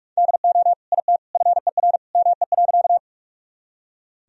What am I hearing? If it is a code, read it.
Morse code, heard as a keyed tone, about 46 wpm, 700 Hz. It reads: DO IT FER ME.
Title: Sentences from Top 200 Words and Abbreviations 45wpm